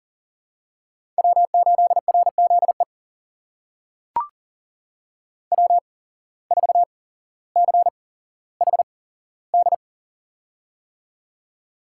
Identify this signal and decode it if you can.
Morse code, heard as a keyed tone, about 39 wpm, spaced as for 17 wpm, 700 Hz. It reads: W8RZE W4CHD
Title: USA Call Signs - 5 Characters - 8x Character Spacing - 40wpm